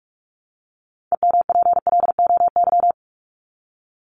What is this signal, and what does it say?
Morse code, heard as a keyed tone, about 44 wpm, 700 Hz. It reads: EMPLOY
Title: Top 1751-2000 Words 45wpm